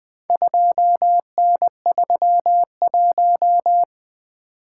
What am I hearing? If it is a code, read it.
Morse code, heard as a keyed tone, about 20 wpm, 700 Hz. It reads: 2N31
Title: Four Letter-Number Combinations - 20wpm